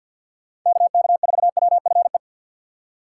Morse code, heard as a keyed tone, about 51 wpm, 700 Hz. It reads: KK4WFE
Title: USA Call Signs - 6 Characters - 3x Character Spacing - 50wpm